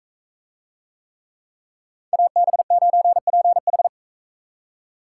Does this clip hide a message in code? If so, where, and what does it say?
Yes: Morse code from 2.1 to 3.9 s, heard as a keyed tone, about 43 wpm, 700 Hz. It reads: AB9PH